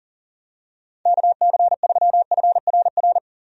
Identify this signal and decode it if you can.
Morse code, heard as a keyed tone, about 38 wpm, 700 Hz. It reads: KC3FRR
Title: USA Call Signs - 6 Characters - 40wpm